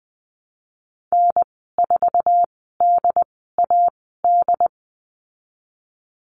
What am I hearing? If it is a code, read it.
Morse code, heard as a keyed tone, about 20 wpm, spaced as for 15 wpm, 700 Hz. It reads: N4DAD
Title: 076 Lesson 13g - Letter L - Call Signs 20wpm